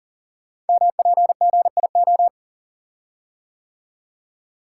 Morse code, heard as a keyed tone, about 39 wpm, 690 Hz. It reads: MPGIO
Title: Code Groups - 5 Letters - 40wpm